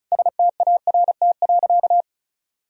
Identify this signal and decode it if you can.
Morse code, heard as a keyed tone, about 34 wpm, 700 Hz. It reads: START.
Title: Sentences from Top 300 Words and Abbreviations 35wpm